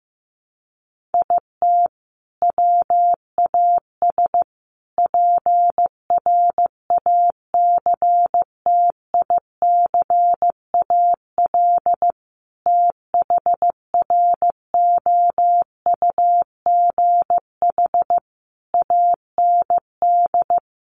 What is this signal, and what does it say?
Morse code, heard as a keyed tone, about 15 wpm, 700 Hz. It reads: IT WAS PRACTICAL THROUGH AND
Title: Sentences from Top 1400 Words 15wpm